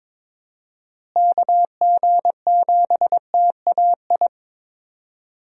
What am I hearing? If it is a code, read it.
Morse code, heard as a keyed tone, about 22 wpm, 700 Hz. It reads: KG7TAI
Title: USA Call Signs - 6 Characters - Repeated 3x - 22wpm